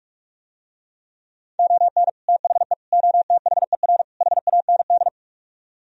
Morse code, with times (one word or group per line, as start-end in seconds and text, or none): 1.6-2.1 ON
2.3-2.7 THE
2.9-4.0 OTHER
4.2-5.1 HAND